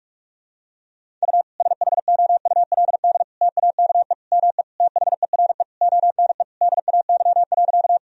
Morse code, heard as a keyed tone, about 41 wpm, 700 Hz. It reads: U SHOULD TAKE ME THERE ONE DAY.
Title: Sentences from Top 300 Words and Abbreviations 45wpm